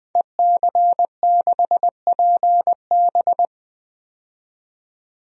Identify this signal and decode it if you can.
Morse code, heard as a keyed tone, about 21 wpm, 700 Hz. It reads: EC6PB
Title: DXCC-100 Call Signs - 5 Characters - 20wpm